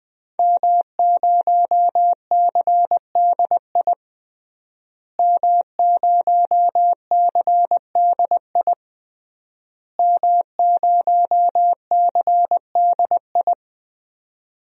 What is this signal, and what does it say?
Morse code, heard as a keyed tone, about 21 wpm, 700 Hz. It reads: M0CDI M0CDI M0CDI
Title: DXCC-100 Call Signs - 5 Characters - Repeated 3x - 20wpm